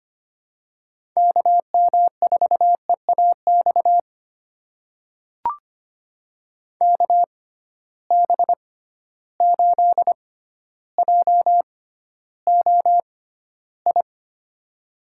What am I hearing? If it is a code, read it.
Morse code, heard as a keyed tone, about 25 wpm, spaced as for 13 wpm, 700 Hz. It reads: KM4EAX KB8JOI
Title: USA Call Signs - 6 Characters - 6x Character Spacing - 25wpm